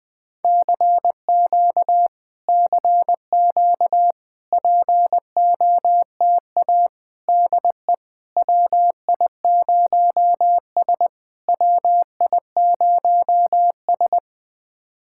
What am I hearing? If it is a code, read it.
Morse code, heard as a keyed tone, about 20 wpm, 700 Hz. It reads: CQ CQ POTA DE WI0S WI0S
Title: POTA QSO Practice - 20wpm